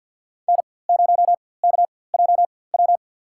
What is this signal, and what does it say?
Morse code, heard as a keyed tone, about 52 wpm, spaced as for 23 wpm, 700 Hz. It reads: N0KJW
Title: USA Call Signs - 5 Characters - 4x Character Spacing - 50wpm